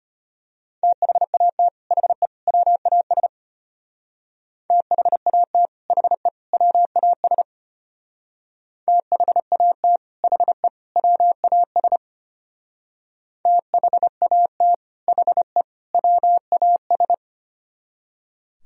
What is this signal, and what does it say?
Morse code, heard as a keyed tone, about 33 wpm, 700 Hz. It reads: THAT HE WAS THAT HE WAS THAT HE WAS THAT HE WAS